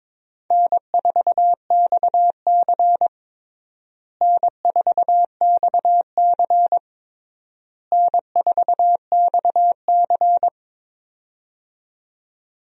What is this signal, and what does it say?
Morse code, heard as a keyed tone, about 22 wpm, 700 Hz. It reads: N4XC N4XC N4XC
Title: USA Call Signs - 4 Characters - Repeated 3x - 22wpm